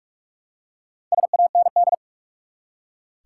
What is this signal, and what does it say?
Morse code, heard as a keyed tone, about 45 wpm, 700 Hz. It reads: SAND
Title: Four Letter Words - 45wpm